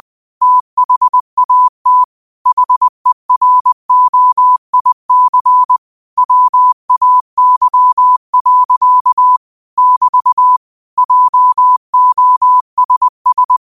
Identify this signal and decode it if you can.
Morse code, heard as a keyed tone, about 20 wpm, 1000 Hz. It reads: THAT HEROIC WAY. <BT> JOSS
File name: 20wpm